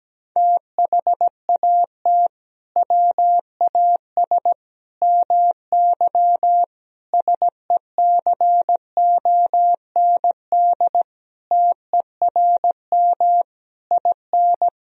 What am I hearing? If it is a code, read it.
Morse code, heard as a keyed tone, about 17 wpm, 700 Hz. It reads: THAT WAS MY SECOND TERM IN
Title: Sentences from Top 1000 Words - Review Part 4 - 17wpm